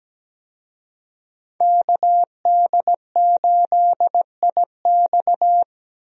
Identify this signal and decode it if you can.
Morse code, heard as a keyed tone, about 17 wpm, 700 Hz. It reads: KD8IX
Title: USA Call Signs - 5 Characters - 17wpm